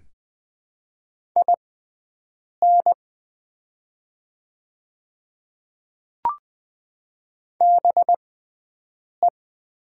Morse code, heard as a keyed tone, about 20 wpm, spaced as for 7 wpm, 700 Hz. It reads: IN BE
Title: 6x Character Spacing - Top 500 Words - 20wpm